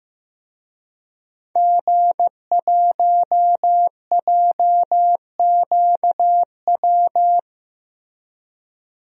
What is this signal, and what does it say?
Morse code, heard as a keyed tone, about 15 wpm, 700 Hz. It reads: G1JQW
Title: DXCC-100 Call Signs - 5 Characters - Repeated 3x - 15wpm